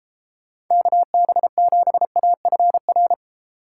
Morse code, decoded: KB7AFR

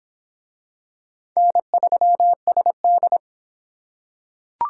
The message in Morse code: N3SD

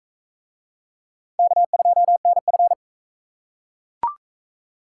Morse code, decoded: K2NF

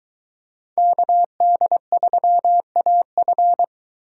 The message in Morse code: KD3AF

